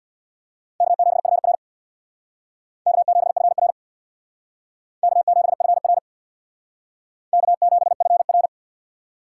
Morse code, heard as a keyed tone, about 61 wpm, 700 Hz. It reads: K7FR K7FR K7FR K7FR
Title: Speed Racing - USA 1x2 Call Signs 50wpm